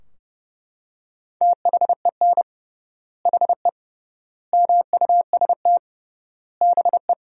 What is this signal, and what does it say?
Morse code, heard as a keyed tone, about 30 wpm, 700 Hz. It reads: THEN HE MUST BE
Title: 3x Word-Spacing - Sentences from Top 500 Words - 30wpm